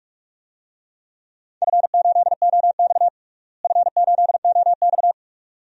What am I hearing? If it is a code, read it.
Morse code, heard as a keyed tone, about 41 wpm, 700 Hz. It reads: F8OX F8OX
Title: DXCC-100 Call Signs - 4 Characters - Repeated 2x - 45wpm